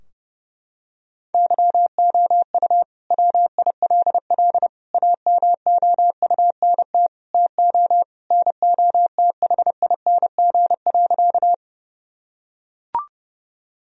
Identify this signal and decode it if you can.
Morse code, heard as a keyed tone, about 30 wpm, 700 Hz. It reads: YOU WILL AMOUNT TO NOTHING.
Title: Sentences from Top 1000 Words - Review Part 4 - 30wpm